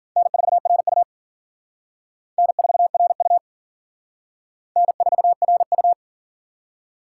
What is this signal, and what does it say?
Morse code, heard as a keyed tone, about 52 wpm, 700 Hz. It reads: N4RU N4RU N4RU